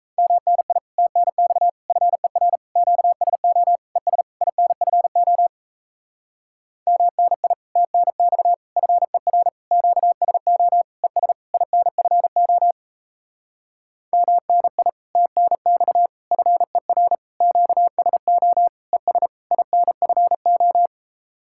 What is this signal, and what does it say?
Morse code, heard as a keyed tone, about 38 wpm, 700 Hz. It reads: MNI TNX FER QSO ES INFO MNI TNX FER QSO ES INFO MNI TNX FER QSO ES INFO